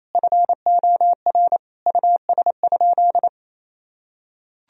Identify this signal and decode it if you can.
Morse code, heard as a keyed tone, about 28 wpm, 700 Hz. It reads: FOR US?
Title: Sentences from Top 1000 Words - Review Part 3 - 28wpm